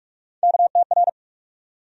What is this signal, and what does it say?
Morse code, heard as a keyed tone, about 42 wpm, 700 Hz. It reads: KTR
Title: USA Call Sign Postfixes - 3 Letters 45wpm